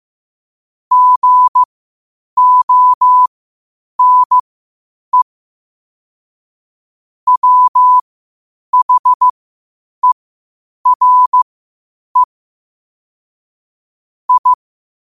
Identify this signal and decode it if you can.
Morse code, heard as a keyed tone, about 15 wpm, spaced as for 8 wpm, 1000 Hz. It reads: GONE WHERE I
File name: qotdfiles / 8wpm